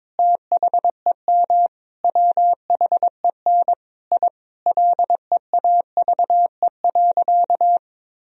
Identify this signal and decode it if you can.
Morse code, heard as a keyed tone, about 23 wpm, 700 Hz. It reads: THEM WHEN I LEAVE.
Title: Sentences from Top 1000 Words - Review Part 2 - 22wpm